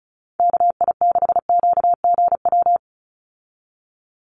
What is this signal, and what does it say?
Morse code, heard as a keyed tone, about 36 wpm, 700 Hz. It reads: KI6QGW
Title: USA Call Signs 35wpm